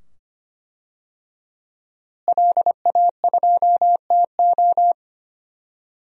Morse code, heard as a keyed tone, about 25 wpm, 700 Hz. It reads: LA2TO
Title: DXCC-100 Call Signs - 5 Characters - Repeated 2x - 25wpm